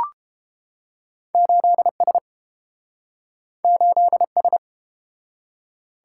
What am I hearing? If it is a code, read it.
Morse code, heard as a keyed tone, about 30 wpm, 700 Hz. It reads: 8S 8S